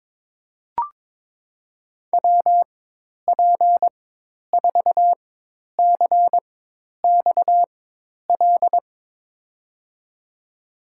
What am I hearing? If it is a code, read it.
Morse code, heard as a keyed tone, about 21 wpm, spaced as for 10 wpm, 700 Hz. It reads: WP4CXL
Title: USA Call Signs - 6 Characters - 4x Character Spacing - 22wpm